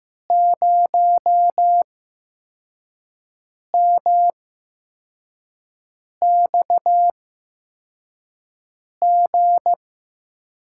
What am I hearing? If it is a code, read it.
Morse code, heard as a keyed tone, about 15 wpm, spaced as for 4 wpm, 700 Hz. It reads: 0MXG
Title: USA Call Signs - 5 Characters - 8x Character Spacing - 15wpm